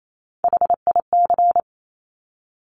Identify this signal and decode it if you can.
Morse code, heard as a keyed tone, about 28 wpm, 700 Hz. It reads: HIC
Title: Top Three Letter Combinations in Words 28wpm